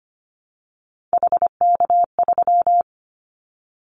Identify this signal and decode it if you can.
Morse code, heard as a keyed tone, about 25 wpm, 700 Hz. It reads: HK3
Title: Callsign Prefixes for Top 100 DXCC - 25wpm